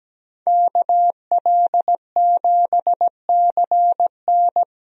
Morse code, heard as a keyed tone, about 17 wpm, 700 Hz. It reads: KL7CN
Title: USA Call Signs - 5 Characters - Repeated 2x - 17wpm